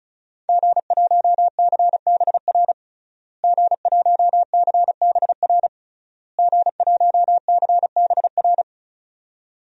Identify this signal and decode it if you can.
Morse code, heard as a keyed tone, about 36 wpm, 700 Hz. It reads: G1CBR G1CBR G1CBR
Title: DXCC-100 Call Signs - 5 Characters - Repeated 3x - 35wpm